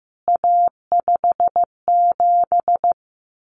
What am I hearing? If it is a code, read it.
Morse code, heard as a keyed tone, about 15 wpm, 700 Hz. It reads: A57
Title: CWT Exchange 15wpm